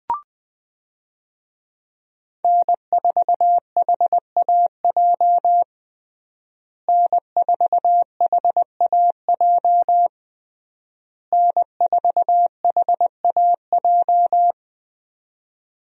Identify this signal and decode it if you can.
Morse code, heard as a keyed tone, about 21 wpm, 700 Hz. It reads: N4HAJ N4HAJ N4HAJ
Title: USA Call Signs - 5 Characters - Repeated 3x - 20wpm